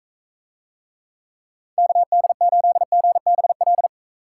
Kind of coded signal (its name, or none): Morse code